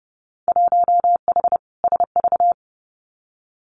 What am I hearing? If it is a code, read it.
Morse code, heard as a keyed tone, about 32 wpm, 700 Hz. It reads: 1H SV
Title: Winter Field Day Exchanges 30wpm